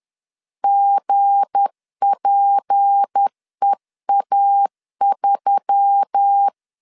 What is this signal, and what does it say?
Morse code, heard as a keyed tone, about 10 wpm, 800 Hz. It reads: GPEA3
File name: LEZIONI / 7 GF9Z / 7 GF9Z 10wpm 2611